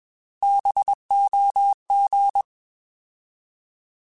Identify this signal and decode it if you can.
Morse code, heard as a keyed tone, about 21 wpm, 800 Hz. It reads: BOG
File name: Alphabetical / Alphabetical mp3s / Bs @20